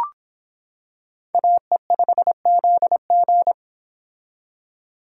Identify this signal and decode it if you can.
Morse code, heard as a keyed tone, about 25 wpm, 700 Hz. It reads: AE5ZG